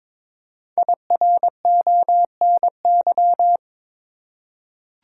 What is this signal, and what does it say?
Morse code, heard as a keyed tone, about 22 wpm, 700 Hz. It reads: IRONY
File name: 5 Letter Words - First 3 Letters Given - 22wpm